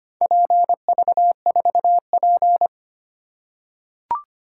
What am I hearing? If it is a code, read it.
Morse code, heard as a keyed tone, about 25 wpm, 690 Hz. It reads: PV4P
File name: practice / 25wpm / Four Letter-Number Combinations - 25wpm